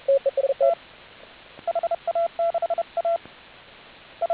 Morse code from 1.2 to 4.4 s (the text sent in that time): HA6A I